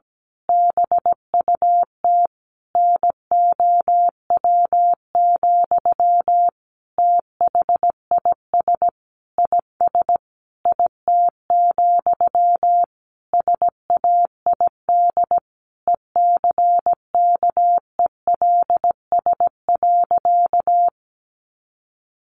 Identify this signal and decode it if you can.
Morse code, heard as a keyed tone, about 17 wpm, 700 Hz. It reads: BUT NOW, THIS IS IT, SAID ECKELS.
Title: A Sound Of Thunder 17wpm